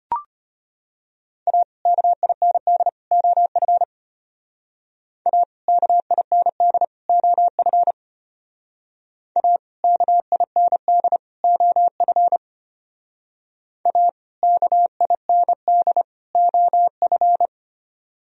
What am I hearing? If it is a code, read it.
Morse code, heard as a keyed tone, about 30 wpm, 700 Hz. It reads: A KIND OF A KIND OF A KIND OF A KIND OF
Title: Speed-Racing - Sets of 3 Words 25wpm